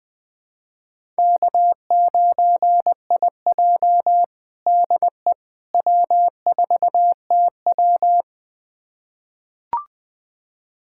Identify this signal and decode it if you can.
Morse code, heard as a keyed tone, about 20 wpm, 700 Hz. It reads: K9IJ DE W4TW
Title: USA Callsign DE Callsign - 4 Characters - 20wpm